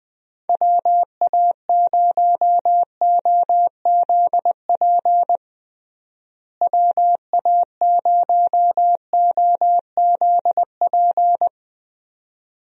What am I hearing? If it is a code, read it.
Morse code, heard as a keyed tone, about 21 wpm, 700 Hz. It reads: WA0OZP WA0OZP
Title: USA Call Signs - 6 Characters - Repeated 2x - 20wpm